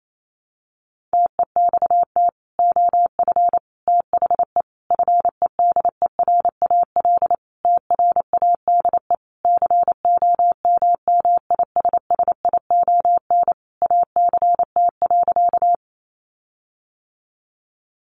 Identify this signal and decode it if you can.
Morse code, heard as a keyed tone, about 28 wpm, 700 Hz. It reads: TEXT OF THE FEDERAL TRADE COMMISSION ACT.
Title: Sentences from Top 1400 Words 28wpm